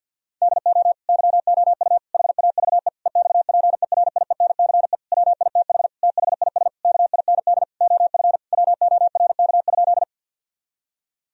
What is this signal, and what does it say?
Morse code, heard as a keyed tone, about 50 wpm, 700 Hz. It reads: DO YOU HAVE EXPERIENCE WITH THIS KIND OF WORK?
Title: Sentences from Top 1000 Words - Review Part 3 - 50wpm